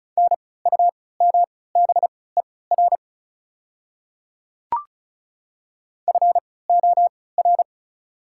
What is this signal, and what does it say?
Morse code, heard as a keyed tone, about 35 wpm, spaced as for 20 wpm, 690 Hz. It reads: NUMBER FOR
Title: 3x Character Spacing - Top 100 Words - 35wpm